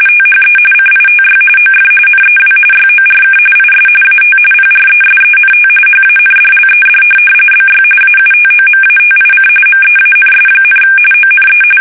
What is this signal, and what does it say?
RTTY, about 45.45 baud. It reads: THE CITY OF NEW YORK IS AT PRESENT HANDLED BY AUTOMATIC MEANS, ONE
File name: fsk850